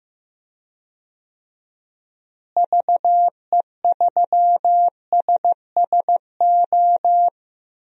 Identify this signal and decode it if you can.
Morse code, heard as a keyed tone, about 15 wpm, 700 Hz. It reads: VE3SSO
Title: DXCC-100 Call Signs - 6 Characters - Repeated 3x - 15wpm